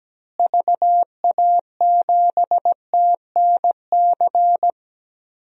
Morse code, heard as a keyed tone, about 17 wpm, 700 Hz. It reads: VA7TNC
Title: DXCC-100 Call Signs - 6 Characters - Repeated 3x - 17wpm